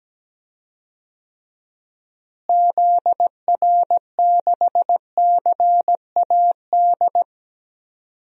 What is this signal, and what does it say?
Morse code, heard as a keyed tone, about 17 wpm, 700 Hz. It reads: ZR6CAD
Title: DXCC-100 Call Signs - 6 Characters - Repeated 2x - 17wpm